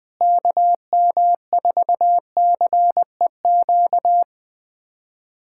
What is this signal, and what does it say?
Morse code, heard as a keyed tone, about 20 wpm, 700 Hz. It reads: KM4CEQ
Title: USA Callsign DE Callsign - 6 Characters - 20wpm